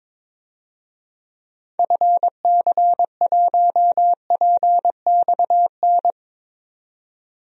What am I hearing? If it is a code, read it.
Morse code, heard as a keyed tone, about 22 wpm, 700 Hz. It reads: FC1PXN
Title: DXCC-100 Call Signs - 6 Characters - 22wpm